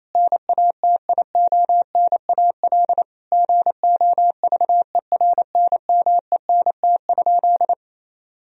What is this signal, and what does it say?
Morse code, heard as a keyed tone, about 28 wpm, 700 Hz. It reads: NATIONAL GOVERNMENT?
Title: Sentences from Top 1000 Words - Review Part 4 - 28wpm